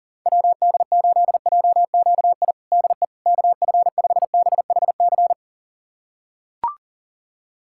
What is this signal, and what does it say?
Morse code, heard as a keyed tone, about 40 wpm, 700 Hz. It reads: WD8JQI DE KF5BHC
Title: USA Callsign DE Callsign 40wpm